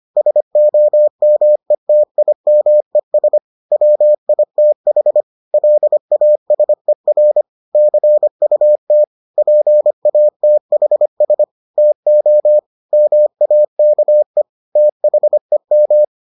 Morse code, heard as a keyed tone, about 25 wpm, 600 Hz. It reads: SOMETIMES WITH LASER CUT PATHS TO MAKE THEM